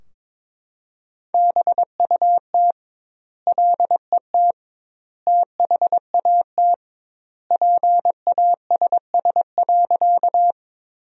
Morse code, heard as a keyed tone, about 22 wpm, 700 Hz. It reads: BUT LET THAT PASS.
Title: 2x Word-Spacing - Sentences from Top 400 Words - 22wpm